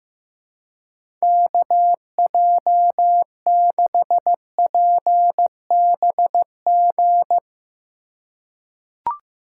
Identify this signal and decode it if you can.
Morse code, heard as a keyed tone, about 16 wpm, 700 Hz. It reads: KJ6PBG
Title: USA Call Signs - 6 Characters - 4x Character Spacing - 15wpm